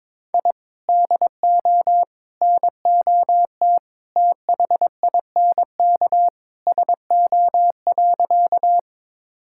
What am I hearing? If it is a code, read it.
Morse code, heard as a keyed tone, about 22 wpm, 700 Hz. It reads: I DO NOT THINK SO.
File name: practice / 22wpm / Sentences with 4-Word Phrases - 22wpm